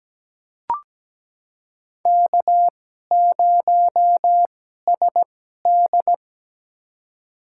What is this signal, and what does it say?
Morse code, heard as a keyed tone, about 17 wpm, spaced as for 12 wpm, 700 Hz. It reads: K0SD
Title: USA Call Signs - 4 Characters - 2x Character Spacing - 17wpm